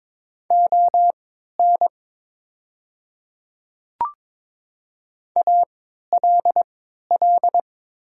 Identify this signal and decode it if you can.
Morse code, heard as a keyed tone, about 22 wpm, spaced as for 12 wpm, 700 Hz. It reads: ON ALL
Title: 3x Character Spacing - Top 100 Words - 22wpm